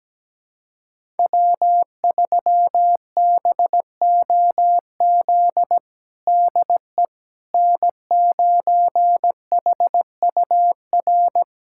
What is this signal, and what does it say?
Morse code, heard as a keyed tone, about 17 wpm, 700 Hz. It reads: W3BOZ DE N9HUR
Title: USA Callsign DE Callsign - 5 Characters - 17wpm